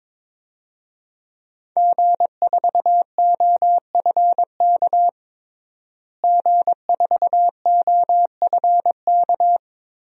Morse code, heard as a keyed tone, about 22 wpm, 700 Hz. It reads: G4OFK G4OFK
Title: DXCC-100 Call Signs - 5 Characters - Repeated 3x - 22wpm